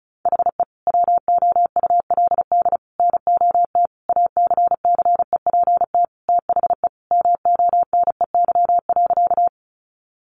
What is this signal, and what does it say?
Morse code, heard as a keyed tone, about 34 wpm, 700 Hz. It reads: HE WOULD NOT ACCEPT THE MONEY.
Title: Sentences from Top 1300 Words 35wpm